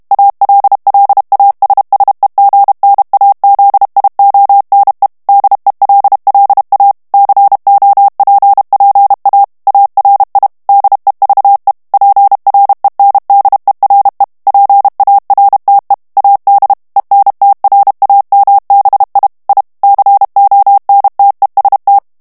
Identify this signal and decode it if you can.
Morse code, heard as a keyed tone, about 32 wpm, 800 Hz. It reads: ALLASSEGNAZIONE DELLA COPPA ARI DEVE PRENDERE PARTE AD ENTRAMBI I CONTEST
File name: texte clair 2 vitesse 30 mots minute